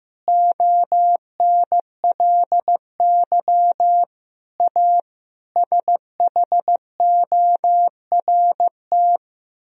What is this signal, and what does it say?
Morse code, heard as a keyed tone, about 15 wpm, 700 Hz. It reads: ONLY A SHORT
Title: Sentences from Top 1000 Words - Review Part 3 - 15wpm